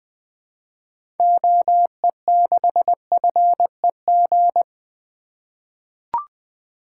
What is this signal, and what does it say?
Morse code, heard as a keyed tone, about 20 wpm, 700 Hz. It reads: OE6FEG
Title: 2023 SOTA Activator Call Signs - 20wpm